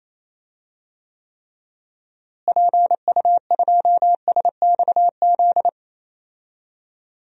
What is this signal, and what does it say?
Morse code, heard as a keyed tone, about 27 wpm, 700 Hz. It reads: PU2SXZ
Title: DXCC-100 Call Signs - 6 Characters - Repeated 2x - 28wpm